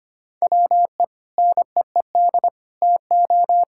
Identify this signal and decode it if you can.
Morse code, heard as a keyed tone, about 25 wpm, 700 Hz. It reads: WE NEED TO
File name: practice / 25wpm / Sentences with 3-Word Phrases - 25wpm